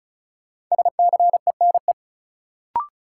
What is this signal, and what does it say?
Morse code, heard as a keyed tone, about 34 wpm, 700 Hz. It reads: SCENE
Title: Sentences from Top 1100 Words - New Word Drill - 35wpm